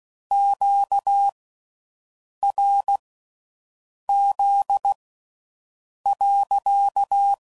Morse code, heard as a keyed tone, about 16 wpm, spaced as for 6 wpm, 800 Hz. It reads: QRZ.